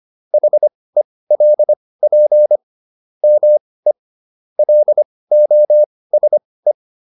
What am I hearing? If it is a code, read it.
Morse code, heard as a keyed tone, about 25 wpm, spaced as for 18 wpm, 600 Hz. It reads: HELP ME LOSE
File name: SS108 18